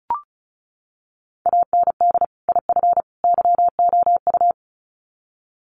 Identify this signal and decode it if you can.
Morse code, heard as a keyed tone, about 34 wpm, 700 Hz. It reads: AND IF YOU
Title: Sets of 3 Words 35wpm